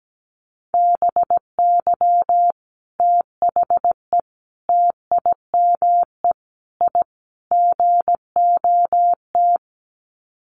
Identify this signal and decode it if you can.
Morse code, heard as a keyed tone, about 17 wpm, 700 Hz. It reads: BY THE TIME I GOT
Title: Sets of 5 Words Encore2 17wpm